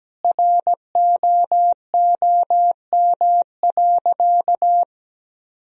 Morse code, heard as a keyed tone, about 17 wpm, 700 Hz. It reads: ROOM.
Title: Sentences from Top 1000 Words - Review Part 1 - 17wpm